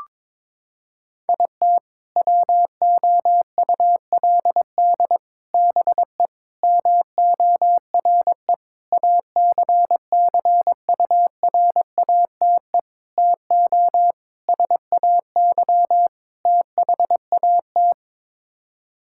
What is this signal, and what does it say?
Morse code, heard as a keyed tone, about 22 wpm, 700 Hz. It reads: IT WOULD BE MORE ACCURATE TO SAY THAT
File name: practice / 22wpm / Sets of 8 Words - 22wpm